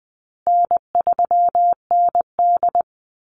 Morse code, heard as a keyed tone, about 20 wpm, 700 Hz. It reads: N3ND
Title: Easy Call Signs Repeated 3x - 20wpm